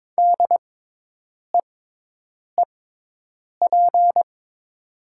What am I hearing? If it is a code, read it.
Morse code, heard as a keyed tone, about 22 wpm, spaced as for 8 wpm, 700 Hz. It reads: DEEP
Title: 6x Character Spacing - Top 500 Words - 22wpm